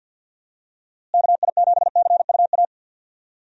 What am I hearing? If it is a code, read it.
Morse code, heard as a keyed tone, about 47 wpm, 700 Hz. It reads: KI7CUA
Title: USA Call Signs - 6 Characters - Repeated 2x - 50wpm